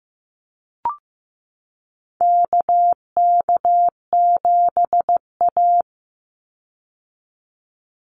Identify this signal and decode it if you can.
Morse code, heard as a keyed tone, about 15 wpm, 700 Hz. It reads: KK7A
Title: Call Signs Repeated 2x - 15wpm